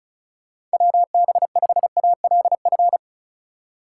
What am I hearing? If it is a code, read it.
Morse code, heard as a keyed tone, about 34 wpm, 700 Hz. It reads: WB5ALF
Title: USA Call Signs - 6 Characters - 4x Character Spacing - 35wpm